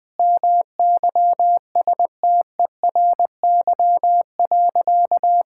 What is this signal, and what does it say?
Morse code, heard as a keyed tone, about 20 wpm, 700 Hz. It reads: MYSTERY.
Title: Sentences with 5-Word Phrases - Spoken Beginning - 20wpm